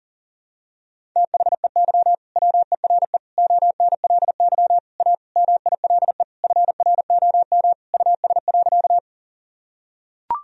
Morse code, heard as a keyed tone, about 40 wpm, 700 Hz. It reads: THEY WERE ONLY A MILE FROM US.
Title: Sentences from Top 1000 Words - Review Part 1 - 40wpm